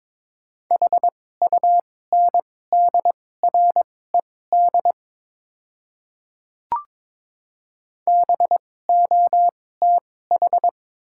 Morse code, heard as a keyed tone, about 22 wpm, spaced as for 16 wpm, 690 Hz. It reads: HUNDRED BOTH
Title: 2x Character Spacing - Top 500 Words - 22wpm